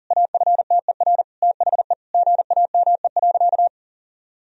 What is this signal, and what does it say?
Morse code, heard as a keyed tone, about 40 wpm, 700 Hz. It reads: AFTER THE GAME.
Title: Sentences from Top 1000 Words - Review Part 4 - 40wpm